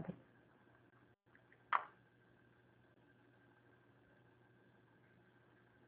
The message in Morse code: none

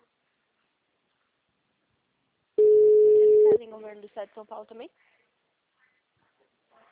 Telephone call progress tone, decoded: ringback tone

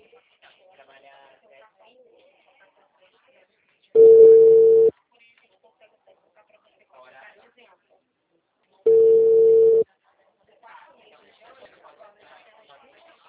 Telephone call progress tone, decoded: ringback tone

